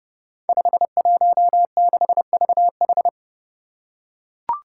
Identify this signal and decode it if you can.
Morse code, heard as a keyed tone, about 31 wpm, 700 Hz. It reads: 516VH
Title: Five Challenging Letter-Number Combinations 30wpm